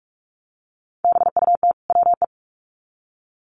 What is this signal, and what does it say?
Morse code, heard as a keyed tone, about 44 wpm, 700 Hz. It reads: BUT WE
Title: Sets of 2 Words 45wpm